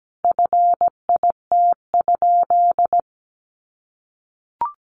Morse code, heard as a keyed tone, about 17 wpm, 700 Hz. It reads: FIT?
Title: Sentences from Top 800 Words 17wpm